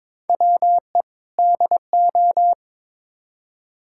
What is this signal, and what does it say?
Morse code, heard as a keyed tone, about 22 wpm, 700 Hz. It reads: WE DO
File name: practice / 22wpm / Build A Phrase - 5-Word Phrases - 22wpm